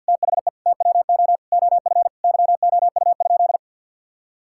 Morse code, heard as a keyed tone, about 47 wpm, 700 Hz. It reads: THE TWO OF YOU?
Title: Sentences from Top 1000 Words - Review Part 6 - 50wpm